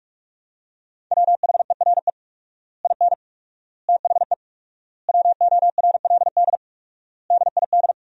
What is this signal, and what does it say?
Morse code, heard as a keyed tone, about 47 wpm, 700 Hz. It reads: WHERE IN THE WORLD DID
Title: 4x Word-Spacing - Sentences from Top 200 Words - 45wpm